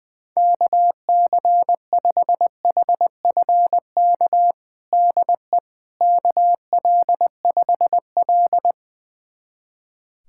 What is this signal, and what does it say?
Morse code, heard as a keyed tone, about 20 wpm, 700 Hz. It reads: KC5HFK DE KL5L
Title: USA Callsign DE Callsign 20wpm